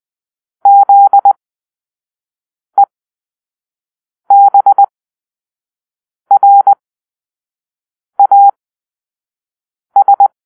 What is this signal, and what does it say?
Morse code, heard as a keyed tone, about 20 wpm, spaced as for 5 wpm, 800 Hz. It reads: ZEBRAS
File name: Grp-5 Drill-2